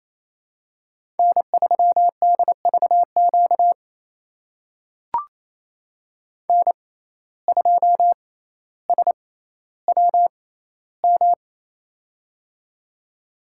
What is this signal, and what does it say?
Morse code, heard as a keyed tone, about 28 wpm, spaced as for 14 wpm, 700 Hz. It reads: N3DVQ N2SWM
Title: USA Call Signs - 5 Characters - 6x Character Spacing - 28wpm